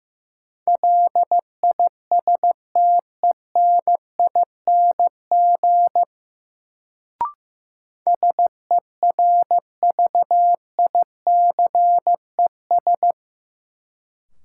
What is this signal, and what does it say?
Morse code, heard as a keyed tone, about 15 wpm, 700 Hz. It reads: LISTENING SERVICES
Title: Sets of 5 Words - New Word Drill - 15wpm